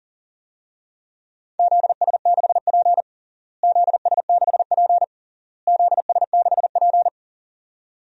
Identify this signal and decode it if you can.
Morse code, heard as a keyed tone, about 40 wpm, 700 Hz. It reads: ZS6P ZS6P ZS6P
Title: DXCC-100 Call Signs - 4 Characters - Repeated 3x - 40wpm